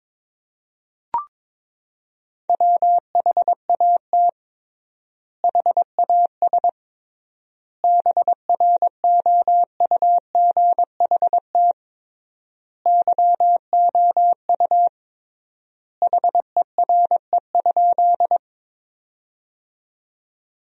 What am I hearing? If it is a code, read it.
Morse code, heard as a keyed tone, about 21 wpm, 700 Hz. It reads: WHAT HAS BROUGHT YOU HERE?
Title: 3x Word-Spacing - Sentences from Top 500 Words - 22wpm